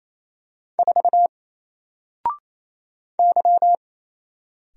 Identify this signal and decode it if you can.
Morse code, heard as a keyed tone, about 28 wpm, 700 Hz. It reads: 4 Y
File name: Single-Letter-Number-28wpm